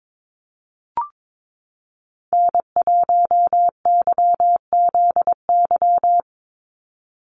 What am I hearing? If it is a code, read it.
Morse code, heard as a keyed tone, about 22 wpm, 700 Hz. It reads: N1YZY
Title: USA Call Signs 22wpm